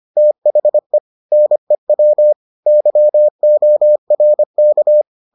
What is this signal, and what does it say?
Morse code, heard as a keyed tone, about 25 wpm, 600 Hz. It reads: THE NEW YORK